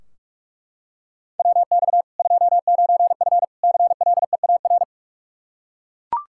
Morse code, heard as a keyed tone, about 41 wpm, 700 Hz. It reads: WX 29F CLEAR
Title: Practice QSO - Exchange 2 - Info - 45wpm